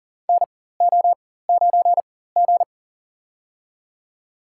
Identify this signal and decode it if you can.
Morse code, heard as a keyed tone, about 40 wpm, spaced as for 19 wpm, 700 Hz. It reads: NO9G
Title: USA Call Signs - 4x Character Spacing - 40wpm